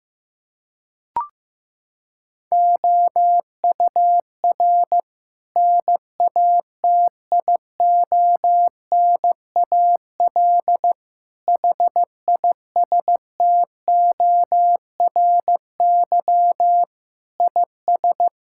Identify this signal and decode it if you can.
Morse code, heard as a keyed tone, about 15 wpm, 700 Hz. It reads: OUR NATIONAL HISTORY IS
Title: Sentences from Top 1000 Words - Review Part 6 - 15wpm